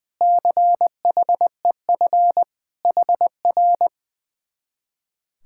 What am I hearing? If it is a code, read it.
Morse code, heard as a keyed tone, about 20 wpm, 700 Hz. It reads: CHEF HR
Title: Speed-Racing Common QSO Phrases - 20wpm